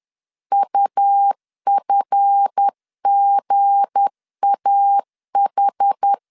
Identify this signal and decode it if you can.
Morse code, heard as a keyed tone, about 10 wpm, 800 Hz. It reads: UFGAH
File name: LEZIONI / 9 2 10wpm 2736